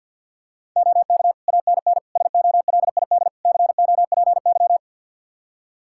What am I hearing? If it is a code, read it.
Morse code, heard as a keyed tone, about 45 wpm, 700 Hz. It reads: OK ANN SOLID COPY
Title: K1USN SST QSO Practice 50wpm